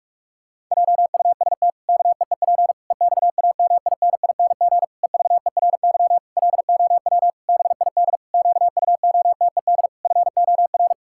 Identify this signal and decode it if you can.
Morse code, heard as a keyed tone, about 46 wpm, 700 Hz. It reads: JUST KEEP EXAMINING EVERY LOW BID QUOTED FOR
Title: Pangram Challenge v2 - 45wpm